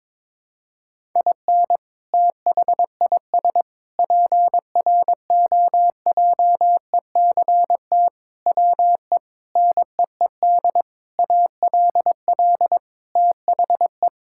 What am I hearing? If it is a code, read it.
Morse code, heard as a keyed tone, about 22 wpm, 700 Hz. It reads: IN THIS PROJECT WE NEED ALL THE
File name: practice / 22wpm / Sentences with 8-Word Phrases - 22wpm